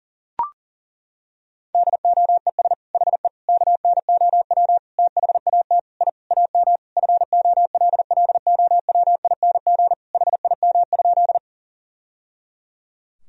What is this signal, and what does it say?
Morse code, heard as a keyed tone, about 40 wpm, 700 Hz. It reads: DOES HE KNOW THAT I AM FOLLOWING HIM?
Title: Sentences from Top 1000 Words - Review Part 4 - 40wpm